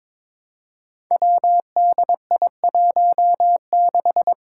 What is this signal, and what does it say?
Morse code, heard as a keyed tone, about 22 wpm, 700 Hz. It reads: WDI16
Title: Five Challenging Letter-Number Combinations 22wpm